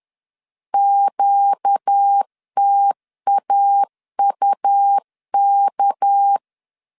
Morse code, heard as a keyed tone, about 10 wpm, 800 Hz. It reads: QTAUK
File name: LEZIONI / 6 PKX0 / 6 PK 10wpm 2436